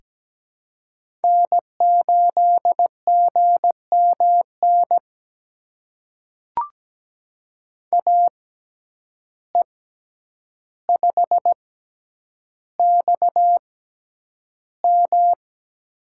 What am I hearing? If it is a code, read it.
Morse code, heard as a keyed tone, about 17 wpm, spaced as for 9 wpm, 700 Hz. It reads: N8GMN AE5XM
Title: USA Call Signs - 5 Characters - 6x Character Spacing - 17wpm